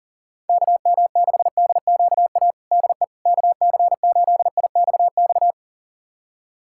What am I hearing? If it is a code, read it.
Morse code, heard as a keyed tone, about 40 wpm, 700 Hz. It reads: KM6DQA DE KC8IXX
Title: USA Callsign DE Callsign 40wpm